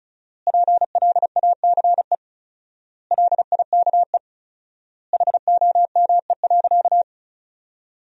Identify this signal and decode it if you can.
Morse code, heard as a keyed tone, about 36 wpm, 700 Hz. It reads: PLACE LIKE HOME.
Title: 4x Word-Spacing - Sentences from Top 200 Words - 35wpm